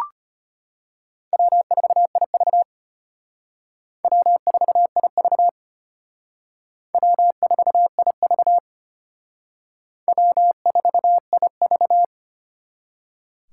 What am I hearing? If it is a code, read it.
Morse code, heard as a keyed tone, about 32 wpm, 700 Hz. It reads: W4IV W4IV W4IV W4IV